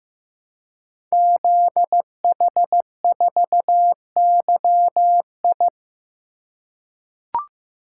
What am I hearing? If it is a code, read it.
Morse code, heard as a keyed tone, about 15 wpm, 700 Hz. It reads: ZH4YI
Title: Code Groups - 5 Letter:Numbers - 15wpm